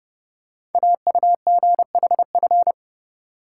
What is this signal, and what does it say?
Morse code, heard as a keyed tone, about 30 wpm, 700 Hz. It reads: AUGHF
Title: Code Groups - 5 Letters - 30wpm